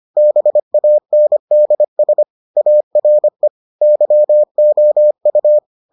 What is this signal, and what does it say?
Morse code, heard as a keyed tone, about 25 wpm, 600 Hz. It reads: BANDS ARE YOU